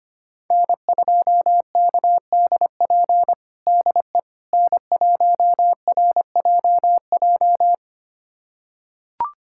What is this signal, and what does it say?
Morse code, heard as a keyed tone, about 25 wpm, 700 Hz. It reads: N2KDP DE N1RJJ
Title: USA Callsign DE Callsign - 5 Characters - 25wpm